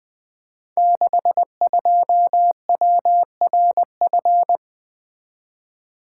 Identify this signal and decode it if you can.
Morse code, heard as a keyed tone, about 20 wpm, 700 Hz. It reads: 62WRF
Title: Code Groups - 5 Letter:Numbers - 20wpm